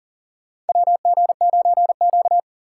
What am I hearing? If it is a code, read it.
Morse code, heard as a keyed tone, about 40 wpm, 700 Hz. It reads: WG9Q
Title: USA Call Signs - 4 Characters - 2x Character Spacing - 40wpm